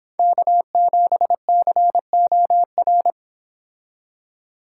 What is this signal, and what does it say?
Morse code, heard as a keyed tone, about 27 wpm, 700 Hz. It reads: K7COR
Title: Speed Racing - USA Call Signs 22wpm